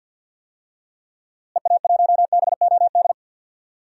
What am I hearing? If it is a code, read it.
Morse code, heard as a keyed tone, about 52 wpm, 700 Hz. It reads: EA1BOD
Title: DXCC-100 Call Signs - 6 Characters - Repeated 2x - 50wpm